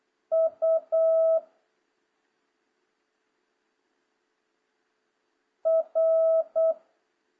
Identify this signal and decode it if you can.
Morse code, heard as a keyed tone, about 8 wpm, 650 Hz. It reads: U R